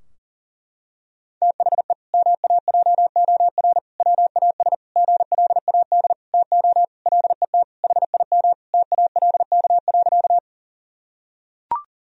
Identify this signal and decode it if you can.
Morse code, heard as a keyed tone, about 40 wpm, 700 Hz. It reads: THE MAJOR WAS GLAD TO LET HIM TALK.
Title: Sentences from Top 1000 Words - Review Part 4 - 40wpm